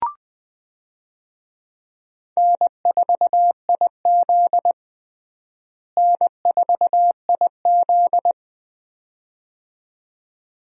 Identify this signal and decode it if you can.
Morse code, heard as a keyed tone, about 19 wpm, 700 Hz. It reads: N4IZ N4IZ
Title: USA Call Signs - 4 Characters - Repeated 2x - 20wpm